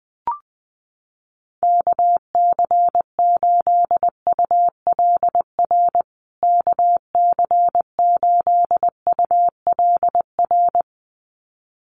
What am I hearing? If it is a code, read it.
Morse code, heard as a keyed tone, about 20 wpm, 700 Hz. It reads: KC8ULR KC8ULR
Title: USA Call Signs Repeated 2x 20wpm